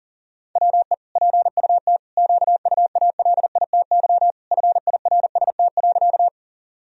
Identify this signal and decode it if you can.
Morse code, heard as a keyed tone, about 40 wpm, 700 Hz. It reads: WE PUT QUALITY FIRST.
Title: Sentences from Top 1000 Words - Review Part 4 - 40wpm